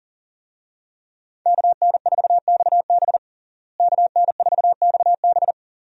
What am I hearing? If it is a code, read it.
Morse code, heard as a keyed tone, about 40 wpm, 700 Hz. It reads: KN4XB KN4XB
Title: USA Call Signs - 5 Characters - Repeated 3x - 40wpm